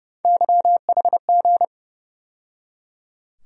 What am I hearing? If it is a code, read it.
Morse code, heard as a keyed tone, about 30 wpm, 700 Hz. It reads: YHG
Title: USA Call Sign Postfixes - 3 Letters 30wpm